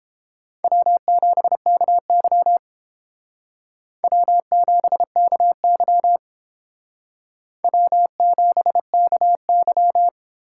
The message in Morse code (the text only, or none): W7KY W7KY W7KY